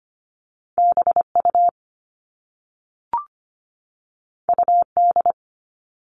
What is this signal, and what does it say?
Morse code, heard as a keyed tone, about 25 wpm, 700 Hz. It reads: BU UD
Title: All Cut Numbers - 2-Digit 25wpm